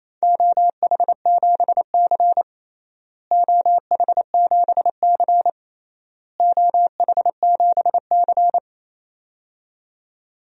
Morse code, heard as a keyed tone, about 28 wpm, 700 Hz. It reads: OH7C OH7C OH7C
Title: DXCC-100 Call Signs - 4 Characters - Repeated 3x - 28wpm